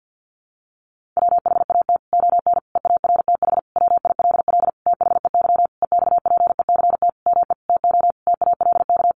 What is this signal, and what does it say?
Morse code, heard as a keyed tone, about 50 wpm, 700 Hz. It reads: WHAT ON EARTH WILL THEY EXPECT ME TO TALK
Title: Sentences from Top 1100 Words 50wpm